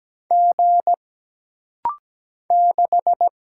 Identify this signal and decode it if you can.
Morse code, heard as a keyed tone, about 17 wpm, 700 Hz. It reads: G 6
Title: Single Letter-Number - Rapid-Fire 17wpm